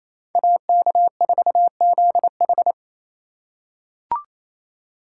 Morse code, heard as a keyed tone, about 28 wpm, 700 Hz. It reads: AK4ZH
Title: USA Call Signs - 5 Characters - 2x Character Spacing - 28wpm